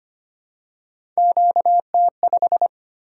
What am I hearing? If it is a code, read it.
Morse code, heard as a keyed tone, about 25 wpm, 700 Hz. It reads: QT5
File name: Three Letter-Number Combinations - 25wpm